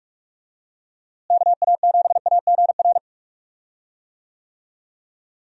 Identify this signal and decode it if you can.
Morse code, heard as a keyed tone, about 46 wpm, 700 Hz. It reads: KA7AGR
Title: USA Call Signs - 6 Characters - Repeated 2x - 45wpm